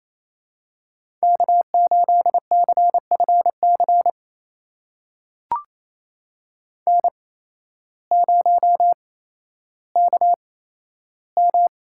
Morse code, heard as a keyed tone, about 28 wpm, spaced as for 13 wpm, 700 Hz. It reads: K8CFC N0KM
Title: USA Call Signs - 5 Characters - 8x Character Spacing - 28wpm